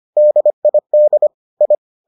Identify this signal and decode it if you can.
Morse code, heard as a keyed tone, about 25 wpm, 600 Hz. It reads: DID I